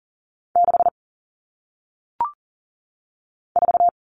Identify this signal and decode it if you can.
Morse code, heard as a keyed tone, about 39 wpm, 700 Hz. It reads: 6 4
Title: Numbers - 1-Digit 40wpm